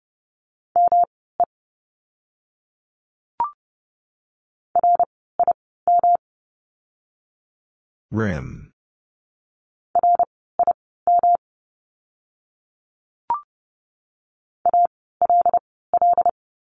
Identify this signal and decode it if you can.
Morse code, heard as a keyed tone, about 30 wpm, spaced as for 17 wpm, 700 Hz. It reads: ME RIM RIM ALL
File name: course / 30:10wpm / 108 Lesson 18f - Letter m - Words 30wpm